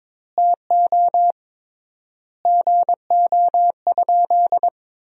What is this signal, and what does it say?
Morse code, heard as a keyed tone, about 22 wpm, 700 Hz. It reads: TO GO?
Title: 3x Word-Spacing - Sentences from Top 300 Words - 22wpm